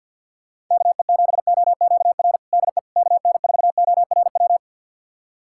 Morse code, heard as a keyed tone, about 47 wpm, 700 Hz. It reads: KE7OQR DE KN4ORW
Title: USA Callsign DE Callsign - 6 Characters - 50wpm